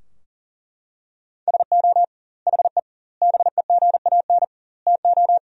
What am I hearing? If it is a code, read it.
Morse code, heard as a keyed tone, about 40 wpm, 700 Hz. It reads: SO HE BEGAN TO
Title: 2x Word-Spacing - Sentences from Top 300 Words - 40wpm